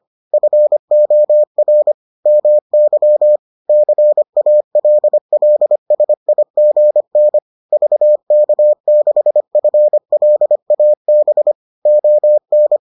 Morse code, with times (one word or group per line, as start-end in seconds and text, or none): 0.3-1.9 FOR
2.3-3.4 MY
3.7-7.4 CALLSIGN
7.7-11.5 VK6FLAB
11.9-12.8 ON